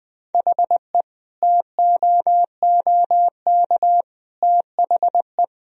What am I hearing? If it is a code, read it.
Morse code, heard as a keyed tone, about 21 wpm, 700 Hz. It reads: HE TOOK THE
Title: Sentences from Top 1000 Words - Review Part 2 - 20wpm